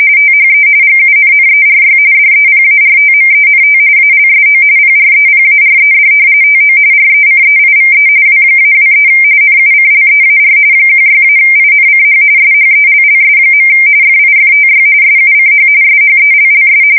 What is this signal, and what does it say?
RTTY, about 45.45 baud. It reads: THE GUNNERY DEPARTMENT ALSO PREPARED TO RECEIVE W-23 NUCLEAR 16-INCH SHELLS. FIFTY SUCH SHELLS
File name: fsk170 / battleshipnj